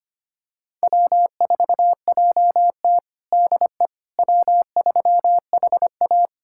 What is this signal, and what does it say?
Morse code, heard as a keyed tone, about 25 wpm, 700 Hz. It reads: W4JT DE W3HA
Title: USA Callsign DE Callsign - 4 Characters - 25wpm